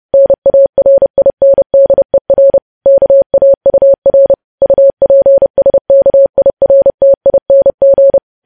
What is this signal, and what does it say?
Morse code, heard as a keyed tone, about 30 wpm, 550 Hz. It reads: NARINDER KAUR UPSKIRTING
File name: podcast / News headlines-1x-30x30